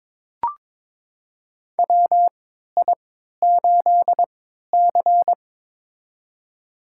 Morse code, heard as a keyed tone, about 23 wpm, spaced as for 13 wpm, 700 Hz. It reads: WI8C
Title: USA Call Signs - 4 Characters - 3x Character Spacing - 22wpm